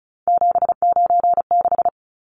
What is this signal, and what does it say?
Morse code, heard as a keyed tone, about 34 wpm, 700 Hz. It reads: 796
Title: Numbers - 3-Digit 35wpm